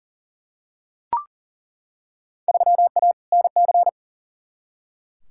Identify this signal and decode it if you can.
Morse code, heard as a keyed tone, about 40 wpm, 700 Hz. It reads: 3A NC
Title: Field Day Exchanges - 40wpm